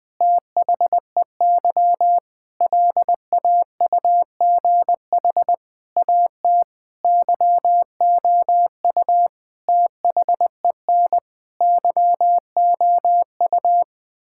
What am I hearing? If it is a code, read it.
Morse code, heard as a keyed tone, about 20 wpm, 700 Hz. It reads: THEY LAUGH AT YOU THEN YOU
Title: Sentences with 8-Word Phrases - 20wpm